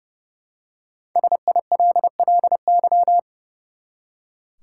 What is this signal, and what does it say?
Morse code, heard as a keyed tone, about 30 wpm, 700 Hz. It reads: SILLY
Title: Five Letter Words - 30wpm